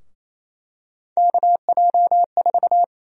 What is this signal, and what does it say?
Morse code, heard as a keyed tone, about 28 wpm, 700 Hz. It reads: KJ4
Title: USA Call Sign Prefixes - 28wpm